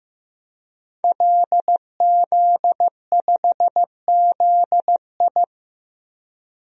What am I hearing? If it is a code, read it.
Morse code, heard as a keyed tone, about 15 wpm, 700 Hz. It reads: LZ5ZI
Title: DXCC-100 Call Signs - 5 Characters - Repeated 2x - 15wpm